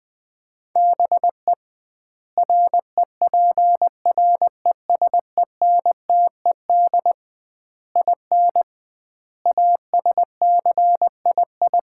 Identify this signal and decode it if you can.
Morse code, heard as a keyed tone, about 20 wpm, 700 Hz. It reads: BE REPRESENTED IN ASCII
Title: 2x Word-Spacing - Sets of 5 Words - 20wpm